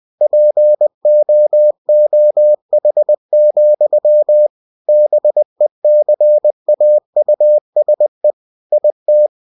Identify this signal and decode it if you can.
Morse code, heard as a keyed tone, about 20 wpm, 600 Hz. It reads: POOH, BECAUSE IT
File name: cw-books / winnie-the-pooh / winnie-20wpm-page-0023